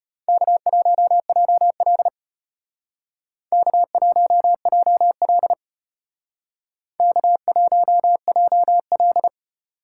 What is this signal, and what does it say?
Morse code, heard as a keyed tone, about 30 wpm, 700 Hz. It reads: K1JL K1JL K1JL
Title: Speed Racing - USA 1x2 Call Signs 25wpm